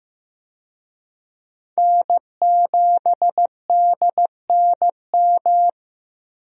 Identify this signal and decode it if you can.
Morse code, heard as a keyed tone, about 15 wpm, 700 Hz. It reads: N7DNM
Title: USA Call Signs - 5 Characters - 15wpm